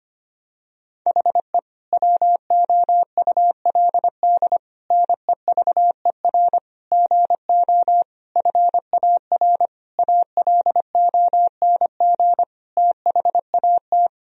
Morse code, heard as a keyed tone, about 25 wpm, 700 Hz. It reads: HE WOULD NEVER GO FAR ALONG THAT
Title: Sentences from Top 1000 Words - Review Part 2 - 25wpm